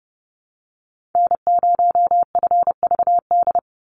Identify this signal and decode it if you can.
Morse code, heard as a keyed tone, about 30 wpm, 700 Hz. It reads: N0FVD
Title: USA Call Signs 30wpm